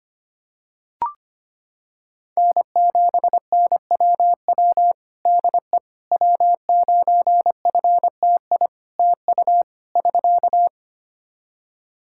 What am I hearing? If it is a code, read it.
Morse code, heard as a keyed tone, about 25 wpm, 700 Hz. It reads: N7NWW DE W9FTI TU <SK>
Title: Practice QSO - Exchange 4 - Ending - 25wpm